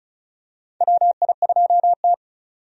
Morse code, heard as a keyed tone, about 36 wpm, 700 Hz. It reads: WI2T
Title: USA Call Signs - 4 Characters - Repeated 3x - 35wpm